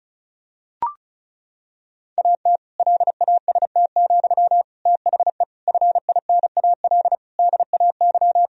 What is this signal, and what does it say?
Morse code, heard as a keyed tone, about 36 wpm, 700 Hz. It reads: AT LAST, THE FINAL DAY
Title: Sentences from Top 1000 Words - Review Part 2 - 35wpm